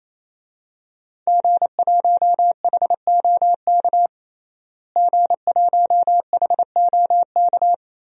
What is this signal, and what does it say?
Morse code, heard as a keyed tone, about 28 wpm, 700 Hz. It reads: G1HOK G1HOK
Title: DXCC-100 Call Signs - 5 Characters - Repeated 3x - 28wpm